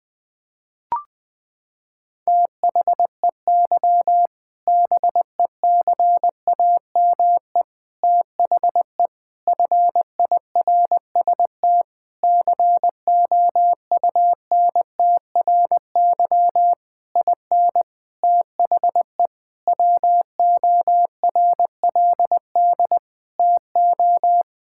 Morse code, heard as a keyed tone, about 21 wpm, 700 Hz. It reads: THEY BECAME THE FIRST COUNTRY IN THE WORLD TO
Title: Sentences with 8-Word Phrases - 20wpm